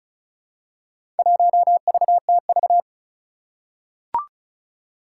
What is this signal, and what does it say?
Morse code, heard as a keyed tone, about 36 wpm, 700 Hz. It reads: 1VTV
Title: Four Challenging Letter-Number Combinations 35wpm